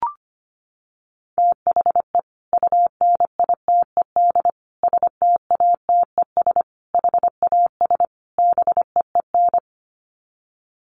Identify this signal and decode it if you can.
Morse code, heard as a keyed tone, about 25 wpm, 700 Hz. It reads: THE UNITED STATES HAS BEEN
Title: Sets of 5 Words Encore2 25wpm